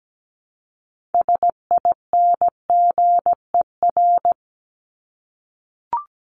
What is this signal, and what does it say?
Morse code, heard as a keyed tone, about 17 wpm, 700 Hz. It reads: SINGER
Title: Top 1501-1750 Words 17wpm